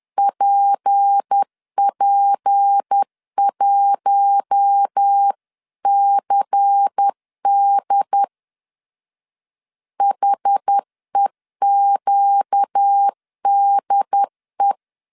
Morse code, heard as a keyed tone, about 10 wpm, 800 Hz. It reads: PP1CD HEQDE